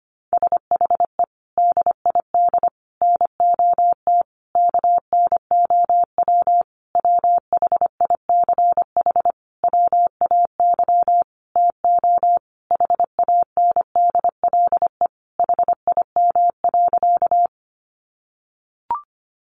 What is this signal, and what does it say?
Morse code, heard as a keyed tone, about 25 wpm, 700 Hz. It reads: SHE DID NOT KNOW WHICH WAY TO HANDLE HIM.
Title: Sentences from Top 1100 Words 25wpm